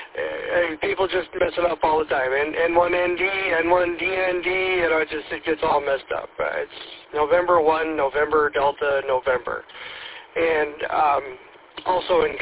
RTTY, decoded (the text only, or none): none